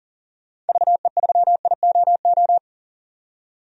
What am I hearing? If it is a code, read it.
Morse code, heard as a keyed tone, about 40 wpm, 700 Hz. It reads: VE3IOO
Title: DXCC-100 Call Signs - 6 Characters - 40wpm